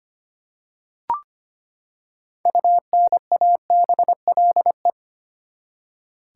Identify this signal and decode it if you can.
Morse code, heard as a keyed tone, about 25 wpm, 700 Hz. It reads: UNABLE
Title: Six Letter Words - 25wpm